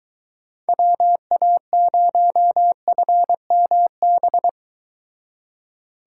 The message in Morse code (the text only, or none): WA0FMB